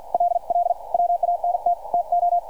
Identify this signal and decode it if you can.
Morse code, heard as a keyed tone, about 50 wpm, 700 Hz. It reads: LL WANT TO